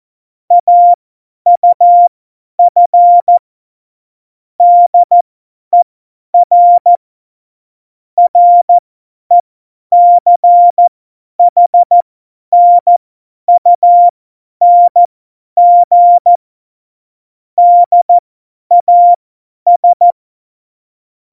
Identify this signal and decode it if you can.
Morse code, heard as a keyed tone, about 14 wpm, spaced as for 10 wpm, 700 Hz. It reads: AUF DER RECHNUNG DAS
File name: Morsekurs / Witze / Witz 04. Februar 2020, 50 F70